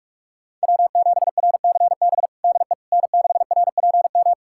Morse code, heard as a keyed tone, about 42 wpm, 700 Hz. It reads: W7RCB DE N6RPM
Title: USA Callsign DE Callsign - 5 Characters - 45wpm